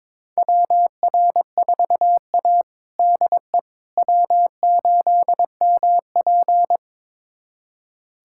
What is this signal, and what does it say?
Morse code, heard as a keyed tone, about 22 wpm, 700 Hz. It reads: WR4A DE W8MP
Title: USA Callsign DE Callsign - 4 Characters - 22wpm